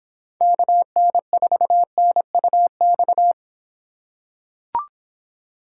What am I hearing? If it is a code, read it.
Morse code, heard as a keyed tone, about 26 wpm, 700 Hz. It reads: KN4NUX